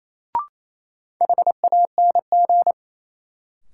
Morse code, heard as a keyed tone, about 28 wpm, 700 Hz. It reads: HANG
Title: Sentences from Top 1600 Words - New Word Drill Alt - 28wpm